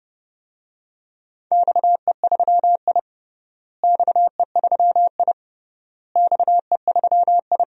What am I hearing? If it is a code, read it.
Morse code, heard as a keyed tone, about 32 wpm, 700 Hz. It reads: XE3I XE3I XE3I
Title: DXCC-100 Call Signs - 4 Characters - Repeated 3x - 30wpm